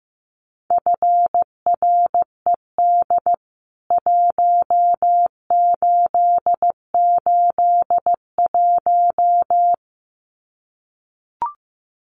Morse code, heard as a keyed tone, about 15 wpm, 700 Hz. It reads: FRED 1881
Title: CWT Exchange 15wpm